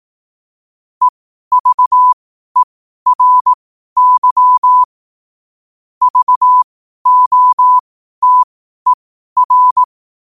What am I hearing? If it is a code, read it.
Morse code, heard as a keyed tone, about 18 wpm, spaced as for 12 wpm, 1000 Hz. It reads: EVERY VOTER
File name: qotdfiles / 12wpm